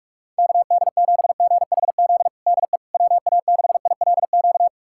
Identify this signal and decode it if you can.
Morse code, heard as a keyed tone, about 46 wpm, 700 Hz. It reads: KD7GHZ DE WA6ILQ
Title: USA Callsign DE Callsign - 6 Characters - 45wpm